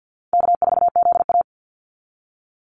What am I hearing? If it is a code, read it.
Morse code, heard as a keyed tone, about 50 wpm, 700 Hz. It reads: K4ZA
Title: Easy Call Signs Repeated 2x - 50wpm